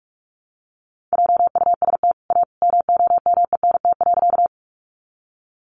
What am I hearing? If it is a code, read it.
Morse code, heard as a keyed tone, about 44 wpm, 700 Hz. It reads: JUST A MOMENT.
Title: Sentences from Top 1000 Words 45wpm